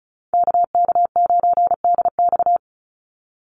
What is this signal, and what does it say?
Morse code, heard as a keyed tone, about 34 wpm, 700 Hz. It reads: KK9DX
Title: Call Signs Repeated 3x - 35wpm